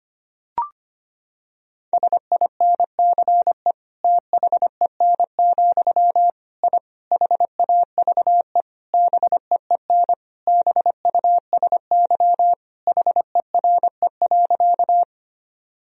Morse code, heard as a keyed tone, about 25 wpm, 700 Hz. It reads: SINCE THEN, I HAVE BEEN BUSY HERE.
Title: Sentences from Top 1000 Words - Review Part 3 - 25wpm